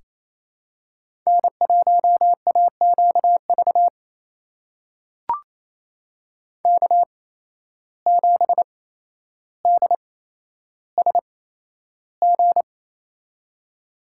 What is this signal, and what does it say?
Morse code, heard as a keyed tone, about 28 wpm, spaced as for 12 wpm, 700 Hz. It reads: N1AQV K7DSG
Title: USA Call Signs - 5 Characters - 8x Character Spacing - 28wpm